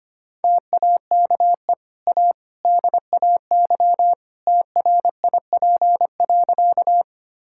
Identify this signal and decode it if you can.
Morse code, heard as a keyed tone, about 25 wpm, 700 Hz. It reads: TAKE A DAY TRIP.
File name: Easy Sentences with Hints - 25wpm